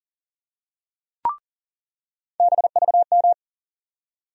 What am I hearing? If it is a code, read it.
Morse code, heard as a keyed tone, about 40 wpm, 700 Hz. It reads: BVM
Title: USA Call Sign Postfixes - 3 Letters 40wpm